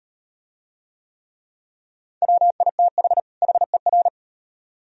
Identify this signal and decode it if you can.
Morse code, heard as a keyed tone, about 38 wpm, 700 Hz. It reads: WITH HER